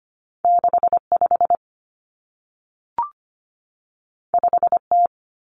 Morse code, heard as a keyed tone, about 25 wpm, 700 Hz. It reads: 65 5T
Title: Common Cut Numbers - 2-Digit 25wpm